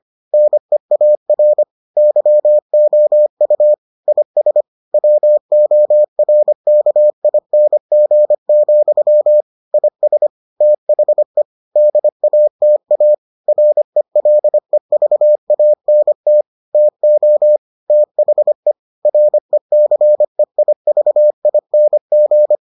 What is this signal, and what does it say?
Morse code, heard as a keyed tone, about 25 wpm, 600 Hz. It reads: NEAR YOU IS WORKING, IS THE DATA RELEVANT TO THE RECEIVING